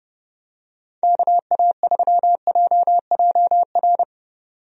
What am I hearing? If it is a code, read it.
Morse code, heard as a keyed tone, about 30 wpm, 700 Hz. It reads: KA3JJR
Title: USA Call Signs - 6 Characters - 30wpm